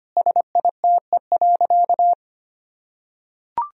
Morse code, heard as a keyed tone, about 25 wpm, 700 Hz. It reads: SITE.
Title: Sentences from Top 1000 Words - Review Part 4 - 25wpm